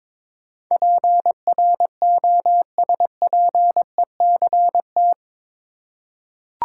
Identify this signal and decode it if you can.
Morse code, heard as a keyed tone, about 22 wpm, 700 Hz. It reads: PROSPECT
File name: practice / 22wpm / Eight Letter Words - 22wpm